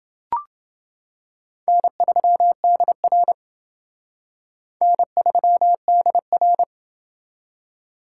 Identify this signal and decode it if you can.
Morse code, heard as a keyed tone, about 28 wpm, 700 Hz. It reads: N3DR N3DR